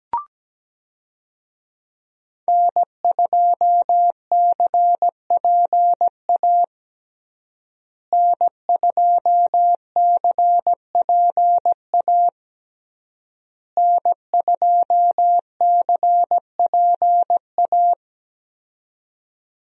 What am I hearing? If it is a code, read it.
Morse code, heard as a keyed tone, about 17 wpm, 700 Hz. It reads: N2CPA N2CPA N2CPA
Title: USA Call Signs - 5 Characters - Repeated 3x - 17wpm